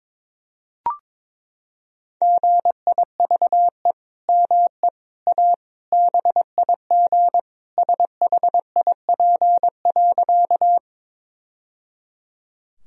Morse code, heard as a keyed tone, about 21 wpm, 700 Hz. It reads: GIVE ME A BIG SHIP.
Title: Sentences from Top 1000 Words - Review Part 2 - 22wpm